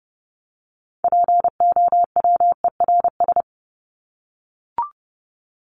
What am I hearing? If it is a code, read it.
Morse code, heard as a keyed tone, about 30 wpm, 700 Hz. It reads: POWERS
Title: ES:S Suffix - Top 100 Words with Base Words 30wpm